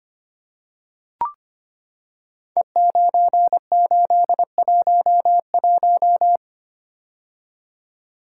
Morse code, heard as a keyed tone, about 25 wpm, 700 Hz. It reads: E9811
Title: Code Groups - 5 Letter:Numbers - 25wpm